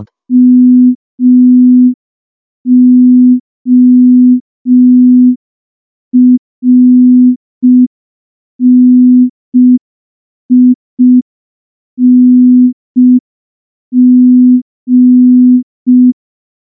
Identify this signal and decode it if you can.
Morse code, heard as a keyed tone, about 5 wpm, 250 Hz. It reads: MORNING
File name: morse / morse-training-05WPM